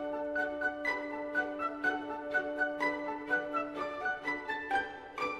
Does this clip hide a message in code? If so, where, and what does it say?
Yes: Emergency Alert System (SAME) tones from 0.6 to 4.4 s, heard as harsh buzzing bursts.